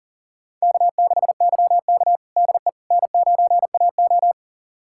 Morse code, heard as a keyed tone, about 40 wpm, 700 Hz. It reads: K6YK DE N9AO
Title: USA Callsign DE Callsign - 4 Characters - 40wpm